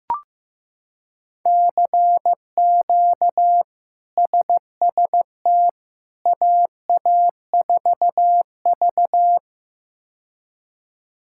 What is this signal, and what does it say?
Morse code, heard as a keyed tone, about 15 wpm, 700 Hz. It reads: CQ SST AA4V
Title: K1USN SST QSO Practice - 15wpm